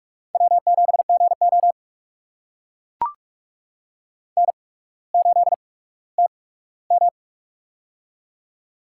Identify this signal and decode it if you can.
Morse code, heard as a keyed tone, about 44 wpm, spaced as for 19 wpm, 700 Hz. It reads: W7GO N8TM
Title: USA Call Signs - 4 Characters - 8x Character Spacing - 45wpm